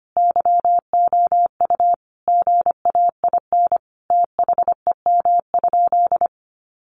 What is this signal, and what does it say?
Morse code, heard as a keyed tone, about 25 wpm, 700 Hz. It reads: YOU GAIN THEM?
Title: Sentences from Top 1000 Words 25wpm